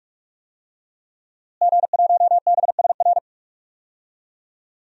Morse code, heard as a keyed tone, about 46 wpm, 700 Hz. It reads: G1BSR
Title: DXCC-100 Call Signs - 5 Characters - Repeated 3x - 45wpm